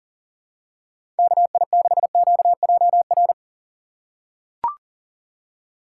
Morse code, heard as a keyed tone, about 40 wpm, 700 Hz. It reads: KI6QJR
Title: USA Call Signs - 3x Character Spacing - 40wpm